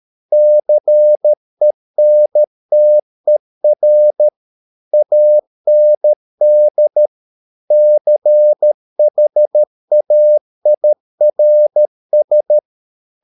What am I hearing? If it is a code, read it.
Morse code, heard as a keyed tone, about 13 wpm, 600 Hz. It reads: CENTER AND CHAIRS